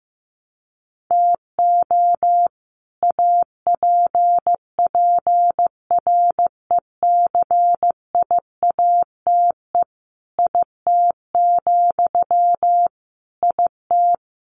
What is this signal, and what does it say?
Morse code, heard as a keyed tone, about 15 wpm, 700 Hz. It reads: TO APPRECIATE IT, IT
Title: Sentences from Top 1300 Words 15wpm